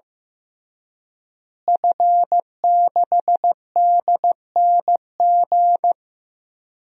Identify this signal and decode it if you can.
Morse code, heard as a keyed tone, about 15 wpm, 700 Hz. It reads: F6DNG
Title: DXCC-100 Call Signs - 5 Characters - Repeated 3x - 15wpm